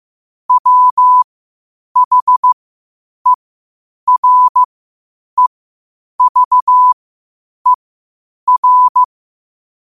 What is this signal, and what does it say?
Morse code, heard as a keyed tone, about 15 wpm, spaced as for 8 wpm, 1000 Hz. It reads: WHEREVER
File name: qotdfiles / 8wpm